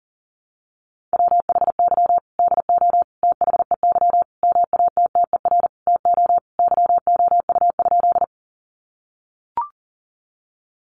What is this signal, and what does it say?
Morse code, heard as a keyed tone, about 40 wpm, 700 Hz. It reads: WHY DO THEY MATTER TO YOU?
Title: Sentences from Top 700 Words 40wpm